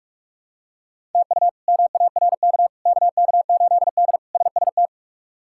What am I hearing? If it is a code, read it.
Morse code, heard as a keyed tone, about 41 wpm, 700 Hz. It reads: TU MARK KK8D SST
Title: K1USN SST QSO Practice - 45wpm